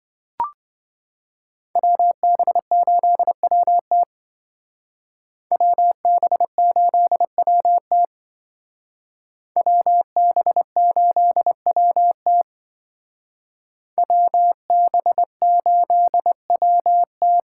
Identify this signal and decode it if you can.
Morse code, heard as a keyed tone, about 26 wpm, 700 Hz. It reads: WB8WT WB8WT WB8WT WB8WT